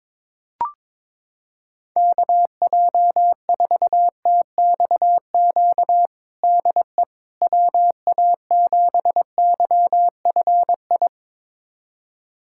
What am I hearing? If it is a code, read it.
Morse code, heard as a keyed tone, about 22 wpm, 700 Hz. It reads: KJ4TXQ DE WA7YFI
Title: USA Callsign DE Callsign - 6 Characters - 22wpm